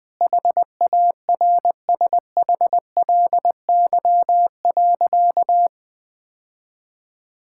Morse code, heard as a keyed tone, about 20 wpm, 700 Hz. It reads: HARSHLY.
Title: Sentences with 6-Word Phrases - 20wpm